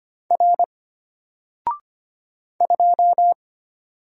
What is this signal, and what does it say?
Morse code, heard as a keyed tone, about 25 wpm, 700 Hz. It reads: R 2
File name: Single-Letter-Number-25wpm